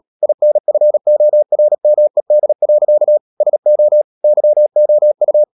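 Morse code, heard as a keyed tone, about 37 wpm, 600 Hz. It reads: INFORMED. SO YOU
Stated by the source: Morse code practice at 35–39 wpm sending English text of Ken Follett's novel Never, from CW med Gnister.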